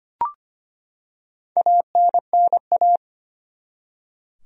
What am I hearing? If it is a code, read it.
Morse code, heard as a keyed tone, about 25 wpm, 700 Hz. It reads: ANNA
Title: USA Common Names - 25wpm